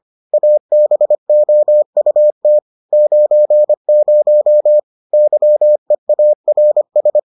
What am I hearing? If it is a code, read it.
Morse code, heard as a keyed tone, about 25 wpm, 600 Hz. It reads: ABOUT 90 YEARS